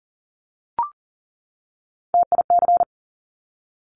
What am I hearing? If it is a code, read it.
Morse code, heard as a keyed tone, about 39 wpm, 700 Hz. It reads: TIC
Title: Top Three Letter Combinations in Words 40wpm